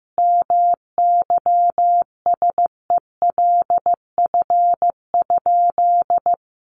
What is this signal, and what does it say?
Morse code, heard as a keyed tone, about 15 wpm, 700 Hz. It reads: MYSELF?
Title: Sentences from Top 1300 Words 15wpm